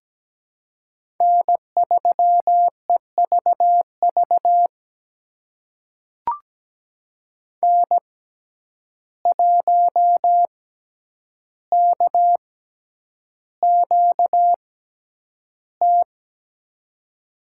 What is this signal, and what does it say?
Morse code, heard as a keyed tone, about 17 wpm, spaced as for 9 wpm, 700 Hz. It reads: N3EVV N1KQT
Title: USA Call Signs - 5 Characters - 6x Character Spacing - 17wpm